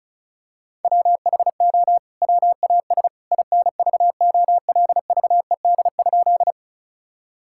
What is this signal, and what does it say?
Morse code, heard as a keyed tone, about 34 wpm, 700 Hz. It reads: WHO WAS INVOLVED?
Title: Sentences from Top 1000 Words - Review Part 5 - 35wpm